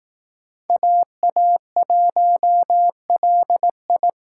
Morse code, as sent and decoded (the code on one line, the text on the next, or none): .- .- .---- .-.. ..
AA1LI